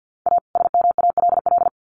Morse code, heard as a keyed tone, about 50 wpm, 700 Hz. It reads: A SMALL
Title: Sets of 2 Words 50wpm